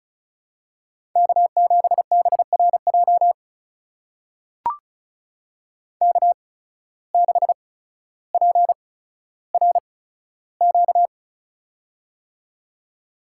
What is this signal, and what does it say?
Morse code, heard as a keyed tone, about 36 wpm, spaced as for 15 wpm, 700 Hz. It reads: K7BRJ K6PRQ
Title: USA Call Signs - 5 Characters - 8x Character Spacing - 35wpm